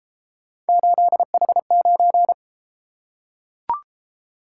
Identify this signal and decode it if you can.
Morse code, heard as a keyed tone, about 33 wpm, 700 Hz. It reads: 8H9